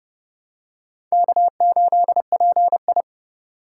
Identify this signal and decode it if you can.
Morse code, heard as a keyed tone, about 29 wpm, 700 Hz. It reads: K8PI